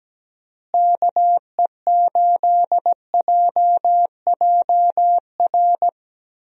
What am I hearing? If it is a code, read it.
Morse code, heard as a keyed tone, about 17 wpm, 700 Hz. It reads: KE8JJR
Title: USA Call Signs - 6 Characters - Repeated 3x - 17wpm